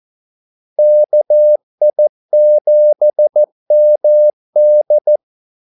Morse code, heard as a keyed tone, about 14 wpm, 600 Hz. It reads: KI7MD
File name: Callsigns-14wpm